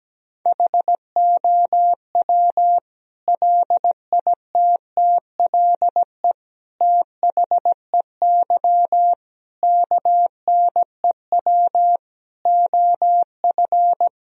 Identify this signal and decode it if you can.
Morse code, heard as a keyed tone, about 17 wpm, 700 Hz. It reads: HOW LITTLE THEY KNEW OF
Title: Sentences from Top 1000 Words - Review Part 1 - 17wpm